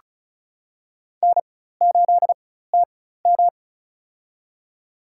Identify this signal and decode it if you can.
Morse code, heard as a keyed tone, about 34 wpm, spaced as for 16 wpm, 700 Hz. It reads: N8TM
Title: USA Call Signs - 4 Characters - 4x Character Spacing - 35wpm